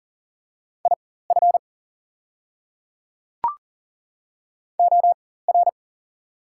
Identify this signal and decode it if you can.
Morse code, heard as a keyed tone, about 40 wpm, spaced as for 19 wpm, 690 Hz. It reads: IF OR
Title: 4x Character Spacing - Top 100 Words - 40wpm